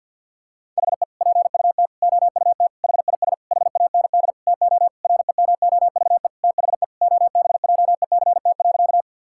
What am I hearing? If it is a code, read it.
Morse code, heard as a keyed tone, about 52 wpm, 690 Hz. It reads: HE PUT OUT HIS HAND TO REMOVE THE OBJECT.
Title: Sentences from Top 1000 Words - Review Part 5 - 50wpm